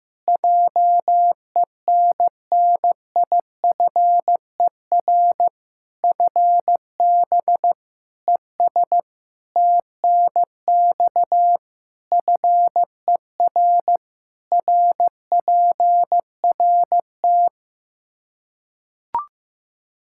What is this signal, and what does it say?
Morse code, heard as a keyed tone, about 16 wpm, 700 Hz. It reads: JENNIFER FB ES TNX FER RPRT
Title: Practice QSO - Exchange 2 - Info - 15wpm